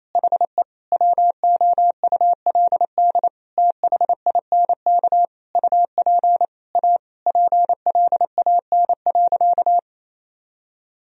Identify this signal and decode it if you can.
Morse code, heard as a keyed tone, about 28 wpm, 700 Hz. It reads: HE WOULD THINK UP A PLAN.
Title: Sentences from Top 1000 Words - Review Part 2 - 28wpm